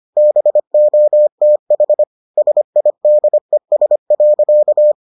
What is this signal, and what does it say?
Morse code, heard as a keyed tone, about 25 wpm, 600 Hz. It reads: BOTH SIDES.